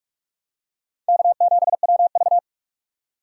Morse code, heard as a keyed tone, about 44 wpm, 700 Hz. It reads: K7WV